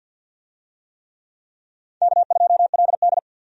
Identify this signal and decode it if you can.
Morse code, heard as a keyed tone, about 51 wpm, 700 Hz. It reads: K2LD
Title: USA Call Signs - 4 Characters - 50wpm